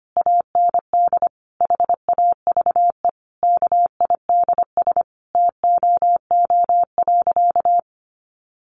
Morse code, heard as a keyed tone, about 25 wpm, 700 Hz. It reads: AND HAVE KIDS TOO.
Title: Easy Phrases 25wpm